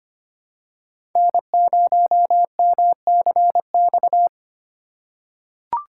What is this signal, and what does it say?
Morse code, heard as a keyed tone, about 25 wpm, 700 Hz. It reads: N0MCX
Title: USA Call Signs - 5 Characters - 3x Character Spacing - 25wpm